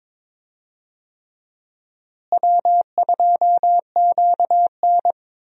Morse code, heard as a keyed tone, about 21 wpm, 700 Hz. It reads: W2QN
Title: USA Call Signs - 4 Characters - Repeated 3x - 22wpm